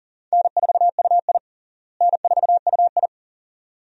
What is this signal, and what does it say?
Morse code, heard as a keyed tone, about 40 wpm, 700 Hz. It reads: N4UI N4UI
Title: USA Call Signs - 4 Characters - Repeated 2x - 40wpm